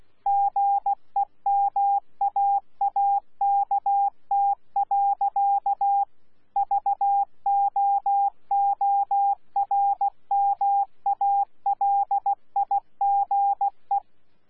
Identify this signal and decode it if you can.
Morse code, heard as a keyed tone, about 16 wpm, 800 Hz. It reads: GEMAAKT. VOORMALIGE
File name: Z PI4VRZ / PI4VRZ 14-jun-2025